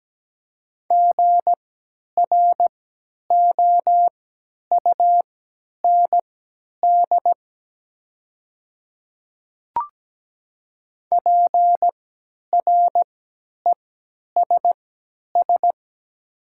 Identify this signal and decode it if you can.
Morse code, heard as a keyed tone, about 17 wpm, spaced as for 10 wpm, 700 Hz. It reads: GROUND PRESS
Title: 3x Character Spacing - Top 500 Words - 17wpm